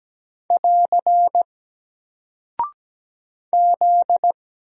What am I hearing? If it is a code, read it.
Morse code, heard as a keyed tone, about 17 wpm, 700 Hz. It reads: <AR> Z
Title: Single Character (Letter:Number:Punctuation:Prosign) 17wpm